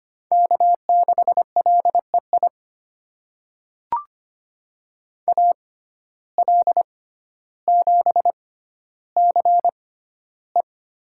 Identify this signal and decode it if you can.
Morse code, heard as a keyed tone, about 25 wpm, spaced as for 13 wpm, 700 Hz. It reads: K6LEI AL7CE
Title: USA Call Signs - 5 Characters - 6x Character Spacing - 25wpm